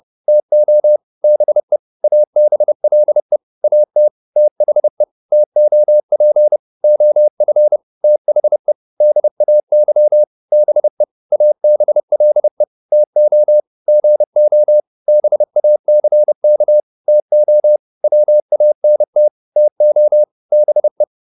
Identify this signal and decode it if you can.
Morse code, heard as a keyed tone, about 30 wpm, 600 Hz. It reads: TO BE ABLE AT THE TOP OF THE DAY BE ABLE TO GO BACK TO WANT TO BE